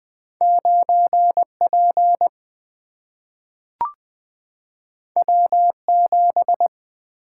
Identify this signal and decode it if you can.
Morse code, heard as a keyed tone, about 20 wpm, 700 Hz. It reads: 9P W7
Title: Single-Letter Single-Number 20wpm